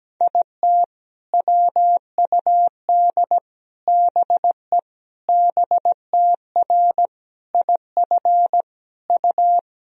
Morse code, heard as a keyed tone, about 17 wpm, 700 Hz. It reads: IT WUD BE BTR IF U
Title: Sentences from Top 400 Words and Abbreviations 17wpm